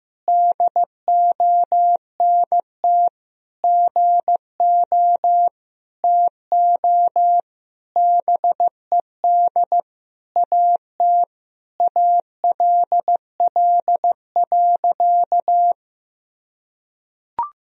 Morse code, heard as a keyed tone, about 15 wpm, 700 Hz. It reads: DONT GO TO BED AT ALL.
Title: Sentences from Top 1000 Words - Review Part 3 - 15wpm